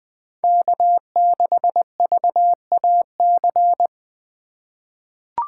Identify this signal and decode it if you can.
Morse code, heard as a keyed tone, about 20 wpm, 700 Hz. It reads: K6VAC
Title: USA Call Signs - 4x Character Spacing - 20wpm